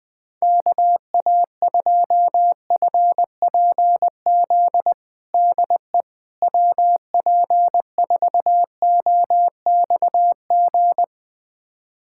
Morse code, heard as a keyed tone, about 21 wpm, 700 Hz. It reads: KA2FPZ DE WP4OXG
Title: USA Callsign DE Callsign - 6 Characters - 20wpm